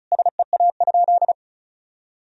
Morse code, heard as a keyed tone, about 35 wpm, 700 Hz. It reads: SEA?
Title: Sentences from Top 1000 Words - Review Part 1 - 35wpm